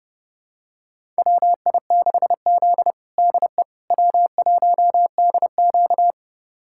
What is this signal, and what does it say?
Morse code, heard as a keyed tone, about 30 wpm, 700 Hz. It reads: WI6Z DE W1DQ
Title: USA Callsign DE Callsign - 4 Characters - 30wpm